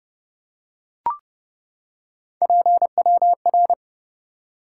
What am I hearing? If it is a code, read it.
Morse code, heard as a keyed tone, about 30 wpm, 700 Hz. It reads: PWR
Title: Top 90 QSO Elements 30wpm